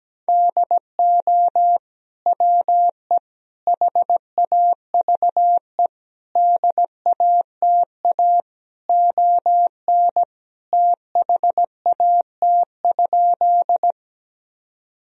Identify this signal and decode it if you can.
Morse code, heard as a keyed tone, about 17 wpm, 700 Hz. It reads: DO WE HAVE DATA ON THAT?
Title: Sentences from Top 1000 Words - Review Part 3 - 17wpm